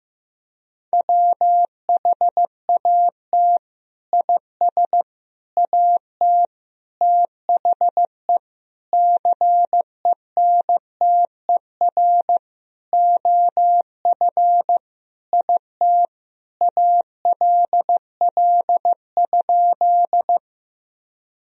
Morse code, heard as a keyed tone, about 15 wpm, 700 Hz. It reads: WHAT IS AT THE CENTER OF IT ALL?
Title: Sentences from Top 1000 Words - Review Part 2 - 15wpm